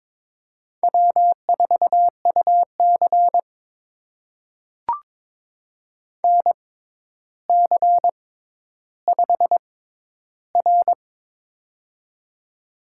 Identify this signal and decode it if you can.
Morse code, heard as a keyed tone, about 22 wpm, spaced as for 11 wpm, 700 Hz. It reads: W4UC NC5R
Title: USA Call Signs - 4 Characters - 6x Character Spacing - 22wpm